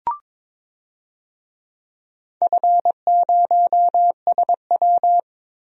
Morse code, heard as a keyed tone, about 22 wpm, 700 Hz. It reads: F0SW
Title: DXCC-100 Call Signs - 4 Characters - Repeated 3x - 22wpm